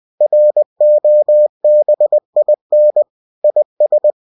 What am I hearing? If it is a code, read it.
Morse code, heard as a keyed tone, about 20 wpm, 600 Hz. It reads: ROBIN IS